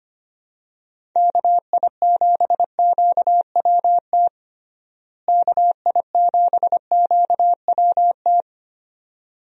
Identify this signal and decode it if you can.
Morse code, heard as a keyed tone, about 25 wpm, 700 Hz. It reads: KI7QWT KI7QWT
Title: USA Call Signs - 6 Characters - Repeated 2x - 25wpm